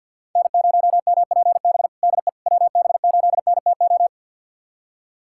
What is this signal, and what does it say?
Morse code, heard as a keyed tone, about 53 wpm, 690 Hz. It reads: N0MPB DE WB8NTO
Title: USA Callsign DE Callsign 50wpm